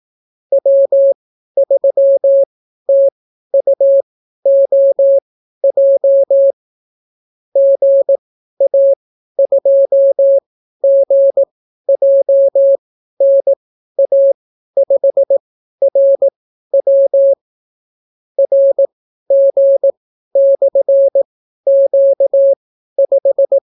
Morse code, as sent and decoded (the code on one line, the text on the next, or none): .-- ...-- - ..- --- .--- / --. .- ..--- --. .--- -. .- ..... .-. .-- / .-. --. -..-. --.- .....
W3TUOJ GA2GJNA5RW RG/Q5